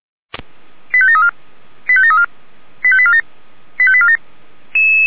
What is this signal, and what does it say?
Five-tone selective call (ZVEI), heard as quick runs of five tones, tones of about 70 ms. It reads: 85734 85734 85847 85847